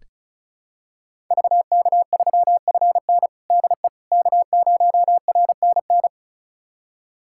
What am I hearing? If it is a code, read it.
Morse code, heard as a keyed tone, about 36 wpm, 700 Hz. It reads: VK3FN DE K0RNN
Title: USA Callsign DE Callsign - 5 Characters - 35wpm